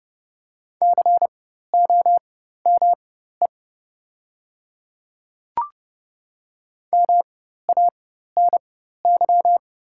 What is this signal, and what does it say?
Morse code, heard as a keyed tone, about 30 wpm, spaced as for 14 wpm, 700 Hz. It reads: COME MANY
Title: 4x Character Spacing - Top 100 Words - 30wpm